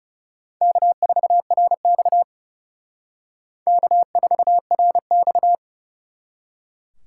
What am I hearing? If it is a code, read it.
Morse code, heard as a keyed tone, about 39 wpm, 700 Hz. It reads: K4RX K4RX